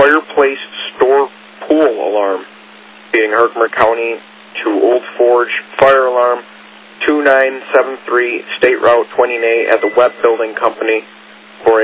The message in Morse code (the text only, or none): none